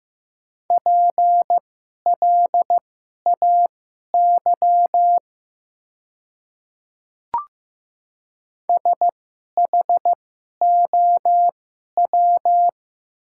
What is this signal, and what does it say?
Morse code, heard as a keyed tone, about 15 wpm, spaced as for 11 wpm, 700 Hz. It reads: PLAY SHOW
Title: 2x Character Spacing - Top 200 Words - 15wpm